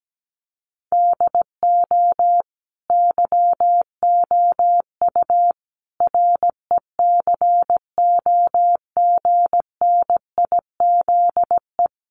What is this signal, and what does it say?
Morse code, heard as a keyed tone, about 17 wpm, 700 Hz. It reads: DO YOU RECOGNIZE
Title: Sentences from Top 1400 Words 17wpm